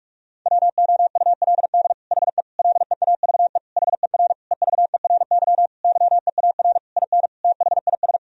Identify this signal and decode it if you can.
Morse code, heard as a keyed tone, about 47 wpm, 700 Hz. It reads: WOULD HE LEAVE HER EVERY YEAR IN THIS
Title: Sentences from Top 1000 Words - Review Part 1 - 45wpm